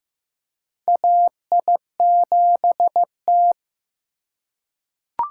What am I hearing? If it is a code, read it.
Morse code, heard as a keyed tone, about 15 wpm, 700 Hz. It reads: AI7T
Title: USA Call Signs - 4 Characters - 2x Character Spacing - 15wpm